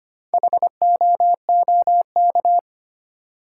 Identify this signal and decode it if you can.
Morse code, heard as a keyed tone, about 25 wpm, 700 Hz. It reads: HOOK
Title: Sentences from Top 1600 Words - New Word Drill - 25wpm